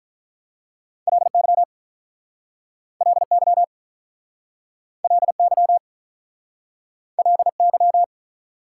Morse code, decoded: LY LY LY LY

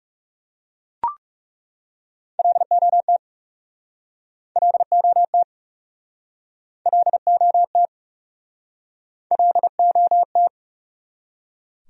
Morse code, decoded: LOT LOT LOT LOT

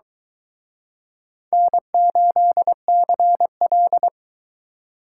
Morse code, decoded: N8CL

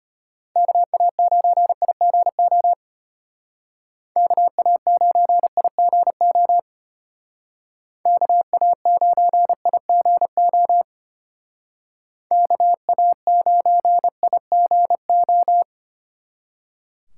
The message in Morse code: KA9IGO KA9IGO KA9IGO KA9IGO

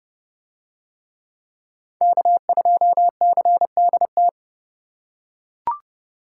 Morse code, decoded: K2CDT

